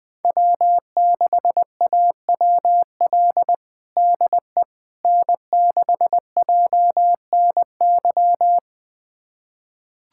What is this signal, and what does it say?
Morse code, heard as a keyed tone, about 20 wpm, 700 Hz. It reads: W6AWL DE N6JNY
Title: USA Callsign DE Callsign - 5 Characters - 20wpm